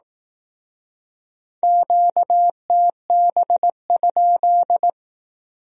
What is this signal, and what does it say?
Morse code, heard as a keyed tone, about 18 wpm, 700 Hz. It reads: QTB?
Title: Speed-Racing Q-Codes - 15wpm